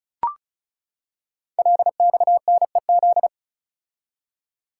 Morse code, heard as a keyed tone, about 36 wpm, 700 Hz. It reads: LXNEZ
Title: Code Groups - 5 Letters - 35wpm